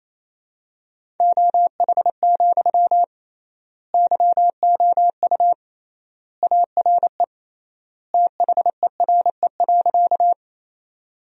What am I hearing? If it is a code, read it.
Morse code, heard as a keyed tone, about 28 wpm, 700 Hz. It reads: OH, YOU ARE THERE.
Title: 3x Word-Spacing - Sentences from Top 500 Words - 28wpm